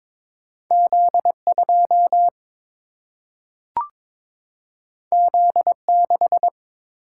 Morse code, heard as a keyed tone, about 22 wpm, 700 Hz. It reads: Z2 Z6
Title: Single-Letter Single-Number 22wpm